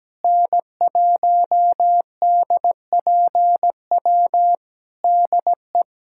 Morse code, heard as a keyed tone, about 17 wpm, 700 Hz. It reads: N1DPW DE
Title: Practice QSO - Exchange 4 - Ending - 17wpm